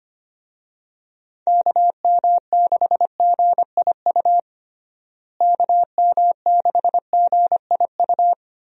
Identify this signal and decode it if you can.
Morse code, heard as a keyed tone, about 25 wpm, 700 Hz. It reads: KM6GIU KM6GIU
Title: USA Call Signs - 6 Characters - Repeated 2x - 25wpm